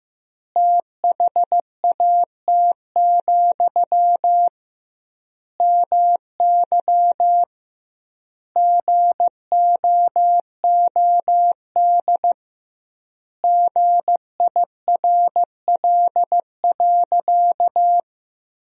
Morse code, heard as a keyed tone, about 16 wpm, 700 Hz. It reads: THAT, MY GOOD GIRL.
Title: 2x Word-Spacing - Sentences from Top 400 Words - 15wpm